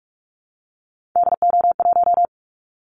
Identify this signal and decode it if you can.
Morse code, heard as a keyed tone, about 44 wpm, 700 Hz. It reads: DO1
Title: Callsign Prefixes for Top 25 DXCC - Repeated 3x - 45wpm